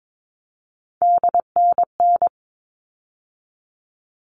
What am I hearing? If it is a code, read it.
Morse code, heard as a keyed tone, about 22 wpm, 700 Hz. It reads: DNN
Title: All Cut Numbers - 3-Digit 22wpm